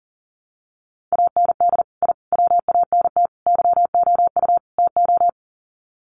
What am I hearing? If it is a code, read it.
Morse code, heard as a keyed tone, about 37 wpm, 700 Hz. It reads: AND I WANT YOU TO
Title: Sets of 5 Words Encore 40wpm